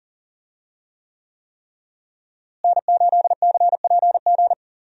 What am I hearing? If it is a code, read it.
Morse code, heard as a keyed tone, about 41 wpm, 700 Hz. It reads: N8CPG
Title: USA Call Signs - 5 Characters - Repeated 3x - 40wpm